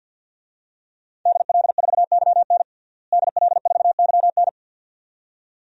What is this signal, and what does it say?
Morse code, heard as a keyed tone, about 52 wpm, 700 Hz. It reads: DL4YN DL4YN
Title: DXCC-100 Call Signs - 5 Characters - Repeated 2x - 50wpm